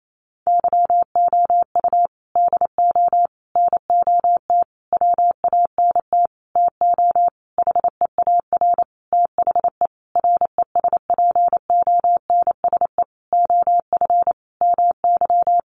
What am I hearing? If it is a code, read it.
Morse code, heard as a keyed tone, about 28 wpm, 700 Hz. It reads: YOU DO NOT WANT TO HEAR THE RESPONSE OF MY
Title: Sentences from Top 1400 Words 28wpm